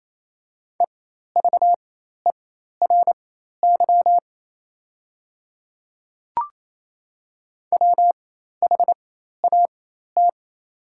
Morse code, heard as a keyed tone, about 28 wpm, spaced as for 13 wpm, 690 Hz. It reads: EVERY WHAT
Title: 4x Character Spacing - Top 200 Words - 28wpm